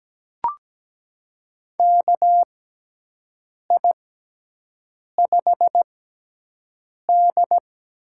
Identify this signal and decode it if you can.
Morse code, heard as a keyed tone, about 17 wpm, spaced as for 6 wpm, 700 Hz. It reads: KI5D
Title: USA Call Signs - 6 Characters - 6x Character Spacing - 17wpm